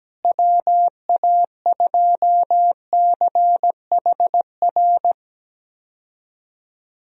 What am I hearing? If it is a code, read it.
Morse code, heard as a keyed tone, about 17 wpm, 700 Hz. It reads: WA2CHR
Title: USA Call Signs - 6 Characters - Repeated 2x - 17wpm